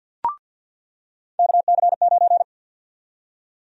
Morse code, heard as a keyed tone, about 46 wpm, 700 Hz. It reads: KC9
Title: USA Call Sign Prefixes - 50wpm